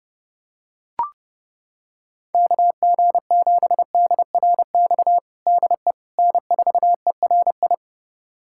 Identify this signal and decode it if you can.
Morse code, heard as a keyed tone, about 30 wpm, 700 Hz. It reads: KG7DRX DE N4ERI
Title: USA Callsign DE Callsign 30wpm